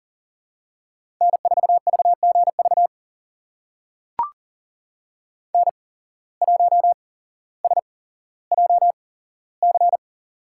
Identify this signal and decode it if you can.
Morse code, heard as a keyed tone, about 38 wpm, spaced as for 17 wpm, 700 Hz. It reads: N4VGV N1SJC
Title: USA Call Signs - 5 Characters - 8x Character Spacing - 40wpm